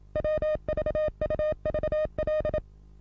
Morse code, heard as a keyed tone, about 31 wpm, 610 Hz. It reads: WVUVL
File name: repos / Morsekurs / DJ4UF Morsen27